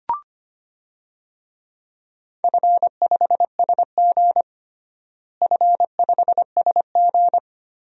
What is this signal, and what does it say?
Morse code, heard as a keyed tone, about 25 wpm, 700 Hz. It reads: F5SG F5SG
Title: DXCC-100 Call Signs - 4 Characters - Repeated 3x - 25wpm